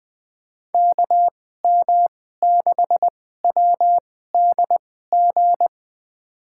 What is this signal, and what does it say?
Morse code, heard as a keyed tone, about 20 wpm, spaced as for 14 wpm, 700 Hz. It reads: KM6WDG
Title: USA Call Signs - 6 Characters - 2x Character Spacing - 20wpm